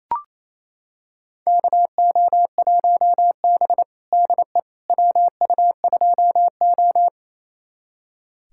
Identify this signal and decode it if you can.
Morse code, heard as a keyed tone, about 28 wpm, 690 Hz. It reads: KO1B DE WU2O
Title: USA Callsign DE Callsign - 4 Characters - 28wpm